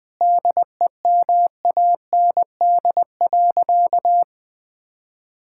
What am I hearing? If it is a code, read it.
Morse code, heard as a keyed tone, about 20 wpm, 700 Hz. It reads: DEMAND.
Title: Sentences from Top 1000 Words - Review Part 5 - 20wpm